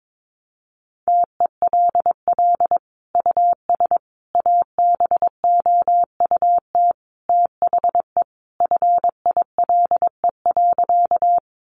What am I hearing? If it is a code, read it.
Morse code, heard as a keyed tone, about 22 wpm, 700 Hz. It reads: TELL US ABOUT THE FILE.
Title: Sentences from Top 1200 Words 22wpm